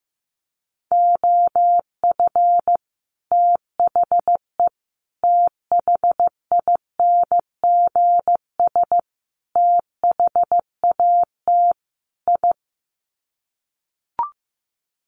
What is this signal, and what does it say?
Morse code, heard as a keyed tone, about 15 wpm, 700 Hz. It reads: OF THE THINGS THAT I
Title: Sets of 5 Words Encore 15wpm